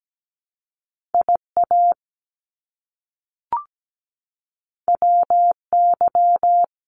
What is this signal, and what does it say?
Morse code, heard as a keyed tone, about 17 wpm, 700 Hz. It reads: IA WY
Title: US States Repeated 3x 17wpm